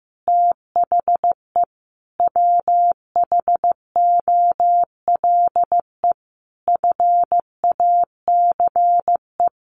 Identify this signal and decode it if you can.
Morse code, heard as a keyed tone, about 15 wpm, 700 Hz. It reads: THE WHOLE FACE
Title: Sentences from Top 1400 Words 15wpm